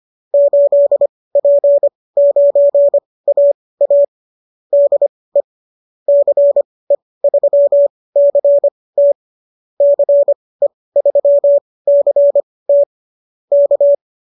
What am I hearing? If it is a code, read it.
Morse code, heard as a keyed tone, about 25 wpm, spaced as for 18 wpm, 590 Hz. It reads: 8P9AA DE CE3CT CE3CT K
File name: qso217 18